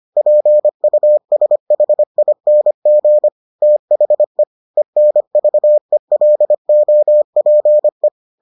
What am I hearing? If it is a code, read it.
Morse code, heard as a keyed tone, about 25 wpm, 600 Hz. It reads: PUSHING THE ENVELOPE